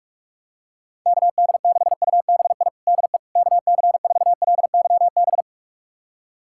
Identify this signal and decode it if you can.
Morse code, heard as a keyed tone, about 41 wpm, 700 Hz. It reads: KD6UBI DE KC4LYB
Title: USA Callsign DE Callsign - 6 Characters - 45wpm